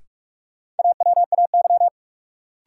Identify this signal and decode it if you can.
Morse code, heard as a keyed tone, about 47 wpm, 700 Hz. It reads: AWAY
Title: Sets of 2 Words - New Word Drill - 45wpm